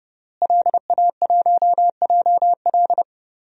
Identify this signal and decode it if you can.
Morse code, heard as a keyed tone, about 31 wpm, 700 Hz. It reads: LA1JL
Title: DXCC-100 Call Signs - 5 Characters - Repeated 2x - 30wpm